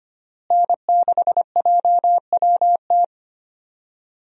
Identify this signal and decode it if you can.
Morse code, heard as a keyed tone, about 25 wpm, 700 Hz. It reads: N6JWT
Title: USA Call Signs - 5 Characters - Repeated 2x - 25wpm